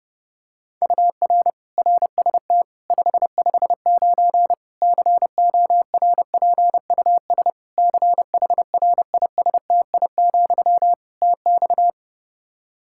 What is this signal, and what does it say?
Morse code, heard as a keyed tone, about 28 wpm, 700 Hz. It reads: UR RST 559 CORPUS CHRISTI, TX
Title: Practice QSO - Exchange 1 - Beginning - 30wpm